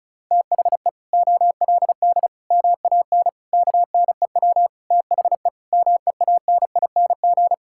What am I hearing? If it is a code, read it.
Morse code, heard as a keyed tone, about 34 wpm, 700 Hz. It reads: THE OLD MAN KNEW THE MEANING
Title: Sentences from Top 1000 Words - Review Part 6 - 35wpm